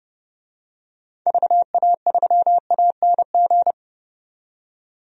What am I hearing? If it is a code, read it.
Morse code, heard as a keyed tone, about 30 wpm, 700 Hz. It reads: VA3ANG
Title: DXCC-100 Call Signs - 6 Characters - 30wpm